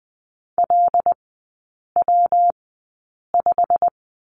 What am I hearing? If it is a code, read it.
Morse code, heard as a keyed tone, about 20 wpm, spaced as for 8 wpm, 700 Hz. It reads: LW5
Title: 114 Lesson 19e - Letter w - Three Characters 20wpm